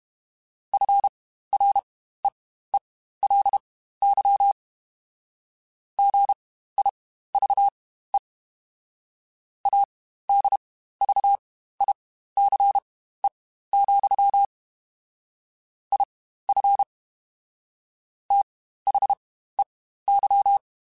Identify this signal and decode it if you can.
Morse code, heard as a keyed tone, about 32 wpm, spaced as for 15 wpm, 800 Hz. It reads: FREELY GIVE ADVICE, IF THEY
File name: Quotes Spaced / Quotes Spaced-mp3s / Advice sp @30